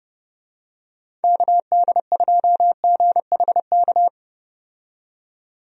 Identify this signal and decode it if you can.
Morse code, heard as a keyed tone, about 30 wpm, 700 Hz. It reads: KD2GHK
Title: USA Call Signs - 6 Characters - 30wpm